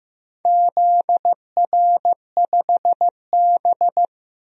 Morse code, heard as a keyed tone, about 15 wpm, 700 Hz. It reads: ZR5B
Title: DXCC-100 Call Signs - 4 Characters - 15wpm